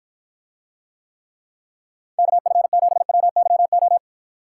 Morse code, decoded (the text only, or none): KF7WYO